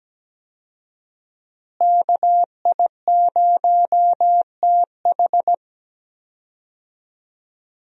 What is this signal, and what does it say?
Morse code, heard as a keyed tone, about 17 wpm, 700 Hz. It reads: KI0TH
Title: USA Call Signs - 5 Characters - Repeated 3x - 17wpm